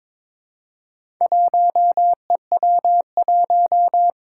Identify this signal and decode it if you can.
Morse code, heard as a keyed tone, about 22 wpm, 700 Hz. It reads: 1EW1
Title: Four Challenging Letter-Number Combinations 22wpm